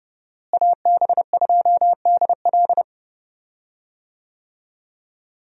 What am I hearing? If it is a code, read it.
Morse code, heard as a keyed tone, about 30 wpm, 700 Hz. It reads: AB2DL
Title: USA Call Signs - 5 Characters - Repeated 2x - 30wpm